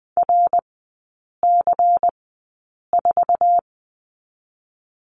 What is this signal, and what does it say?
Morse code, heard as a keyed tone, about 20 wpm, spaced as for 8 wpm, 700 Hz. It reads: RC4
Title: 093 Lesson 16e - Letter c - Three Characters 20wpm